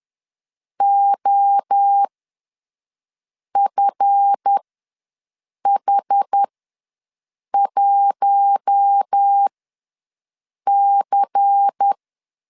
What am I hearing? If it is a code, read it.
Morse code, heard as a keyed tone, about 10 wpm, spaced as for 5 wpm, 800 Hz. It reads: OFH1C
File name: LEZIONI / 7 GF9Z / 7 GF ws 10wpm 3918